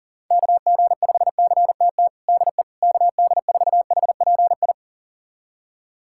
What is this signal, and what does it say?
Morse code, heard as a keyed tone, about 37 wpm, 700 Hz. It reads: KG5CTT DE KD4HPI
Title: USA Callsign DE Callsign 40wpm